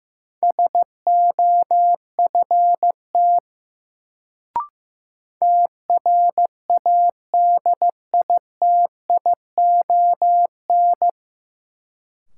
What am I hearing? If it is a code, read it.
Morse code, heard as a keyed tone, about 15 wpm, 700 Hz. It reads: SOFT TRADITION
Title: Sentences from Top 1500 Words - New Word Drill Alt - 15wpm